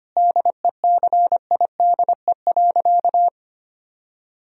Morse code, heard as a keyed tone, about 25 wpm, 700 Hz. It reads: DECIDE.
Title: Sentences from Top 1000 Words - Review Part 3 - 25wpm